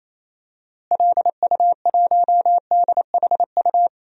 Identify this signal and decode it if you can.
Morse code, heard as a keyed tone, about 28 wpm, 700 Hz. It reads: LU1DHU
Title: DXCC-100 Call Signs - 6 Characters - Repeated 2x - 28wpm